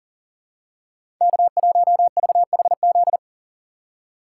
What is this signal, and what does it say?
Morse code, heard as a keyed tone, about 39 wpm, 700 Hz. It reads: K1VHZ
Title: USA Call Signs - 5 Characters - 6x Character Spacing - 40wpm